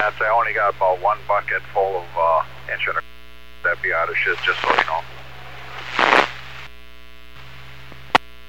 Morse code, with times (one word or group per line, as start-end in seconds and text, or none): none